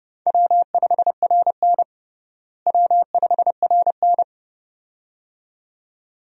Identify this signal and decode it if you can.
Morse code, heard as a keyed tone, about 32 wpm, 700 Hz. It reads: W5RN W5RN
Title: USA Call Signs - 4 Characters - Repeated 2x - 30wpm